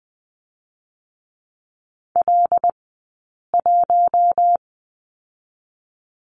Morse code, heard as a keyed tone, about 20 wpm, 700 Hz. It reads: L 1
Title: 073 Lesson 13d - Letter L - Two Characters 20wpm